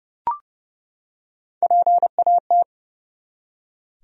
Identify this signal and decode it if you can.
Morse code, heard as a keyed tone, about 30 wpm, 700 Hz. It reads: PAT
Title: USA Common Names - Max 5 Letters - 30wpm